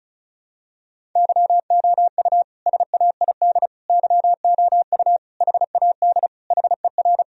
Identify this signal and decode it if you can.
Morse code, heard as a keyed tone, about 34 wpm, 700 Hz. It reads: YOU SAID YOU HAD HER
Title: Sentences from Top 1000 Words - Review Part 1 - 35wpm